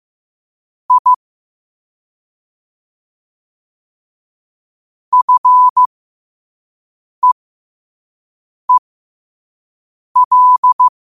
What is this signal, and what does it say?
Morse code, heard as a keyed tone, about 15 wpm, spaced as for 5 wpm, 1000 Hz. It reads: I FEEL